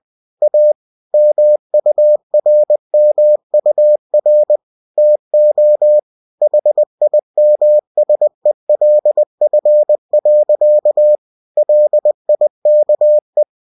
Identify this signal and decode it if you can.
Morse code, heard as a keyed tone, about 20 wpm, 600 Hz. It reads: A MURMUR TO HIMSELF. LIKE
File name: cw-books / winnie-the-pooh / winnie-20wpm-page-0039